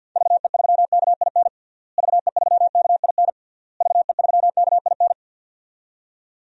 Morse code, heard as a keyed tone, about 46 wpm, 700 Hz. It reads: VE3KIN VE3KIN VE3KIN
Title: DXCC-100 Call Signs - 6 Characters - Repeated 3x - 50wpm